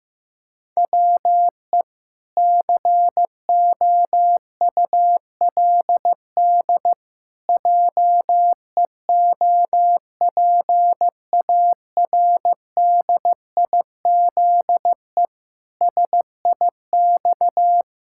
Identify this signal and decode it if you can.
Morse code, heard as a keyed tone, about 15 wpm, 700 Hz. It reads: WE COULD JEOPARDIZE SIX
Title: Pangram Challenge v2 - 15wpm